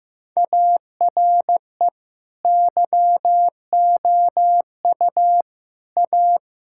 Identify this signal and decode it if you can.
Morse code, heard as a keyed tone, about 15 wpm, 700 Hz. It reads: ARE YOU A
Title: Sentences from Top 1000 Words - Review Part 6 - 15wpm